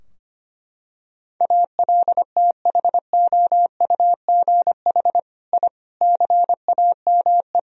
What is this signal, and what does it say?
Morse code, heard as a keyed tone, about 25 wpm, 700 Hz. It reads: ALTHOUGH I CAME
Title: Sentences from Top 1000 Words - Review Part 3 - 25wpm